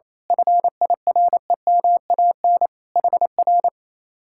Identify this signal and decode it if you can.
Morse code, heard as a keyed tone, about 26 wpm, 700 Hz. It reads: FIREMAN HR
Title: Speed-Racing Common QSO Phrases - 28wpm